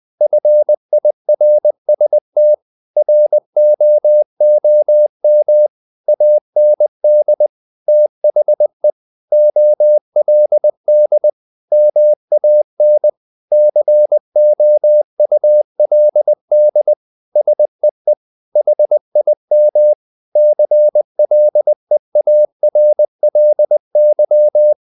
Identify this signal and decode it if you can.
Morse code, heard as a keyed tone, about 20 wpm, 600 Hz. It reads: FIRST ROOM AND THE OLD MAN COULD SEE HIM CLEARLY
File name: the old man and the sea / the old man and the sea part 1 20wpm